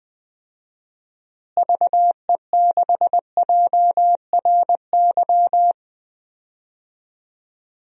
Morse code, heard as a keyed tone, about 21 wpm, 700 Hz. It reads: VE6JRY
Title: DXCC-100 Call Signs - 6 Characters - Repeated 3x - 20wpm